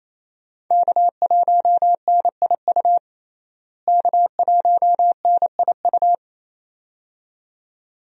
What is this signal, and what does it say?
Morse code, heard as a keyed tone, about 28 wpm, 700 Hz. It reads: K1NIU K1NIU
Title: USA Call Signs - 5 Characters - Repeated 2x - 28wpm